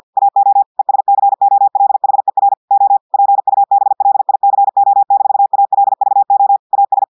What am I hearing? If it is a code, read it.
Morse code, heard as a keyed tone, about 50 wpm, 800 Hz. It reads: AO ESCOLHER O PUBLICO-ALVO AS
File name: audios / portugais / portugais texte en clair 2 vitesse 50 mots minutes